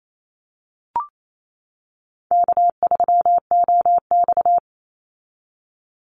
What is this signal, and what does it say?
Morse code, heard as a keyed tone, about 28 wpm, 700 Hz. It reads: K3OX
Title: Easy Call Signs - 28wpm